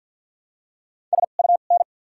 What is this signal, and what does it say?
Morse code, heard as a keyed tone, about 52 wpm, 700 Hz. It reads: S U N
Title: 2x Character Spacing - Top 500 Words - 50wpm